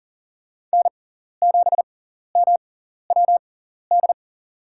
Morse code, heard as a keyed tone, about 40 wpm, spaced as for 14 wpm, 700 Hz. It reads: N7MWD
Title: USA Call Signs - 6x Character Spacing - 40wpm